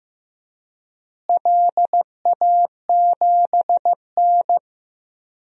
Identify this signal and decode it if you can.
Morse code, heard as a keyed tone, about 15 wpm, 700 Hz. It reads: LA7N
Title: DXCC-100 Call Signs - 4 Characters - Repeated 3x - 15wpm